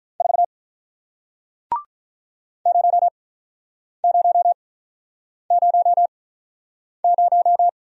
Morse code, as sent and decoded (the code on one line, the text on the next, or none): ....- / ----- ----- ----- -----
4 0000